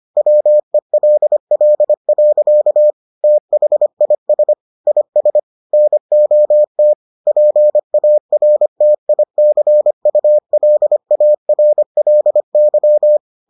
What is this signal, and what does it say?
Morse code, heard as a keyed tone, about 25 wpm, 600 Hz. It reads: WELL. THIS IS NOT PARTICULARLY